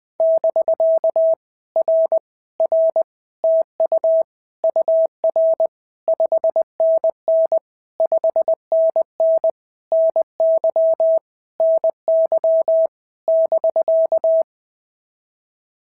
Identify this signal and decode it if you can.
Morse code, heard as a keyed tone, about 20 wpm, 650 Hz. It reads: <BK> R R TU UR 5NN 5NN NY NY <BK>
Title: POTA QSO Practice - 20wpm